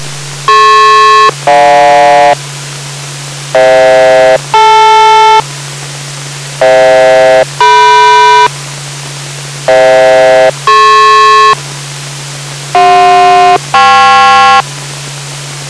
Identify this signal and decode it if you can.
Aircraft SELCAL, heard as two pairs of tones. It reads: DP-GJ FH-DL FH-DM FH-DP CJ-KP